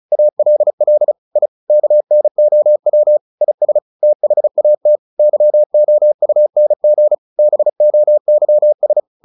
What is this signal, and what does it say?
Morse code, heard as a keyed tone, about 34 wpm, 600 Hz. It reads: ALL I KNOW IS THAT YOUNG BOYS
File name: the old man and the sea / the old man and the sea 1 35wpm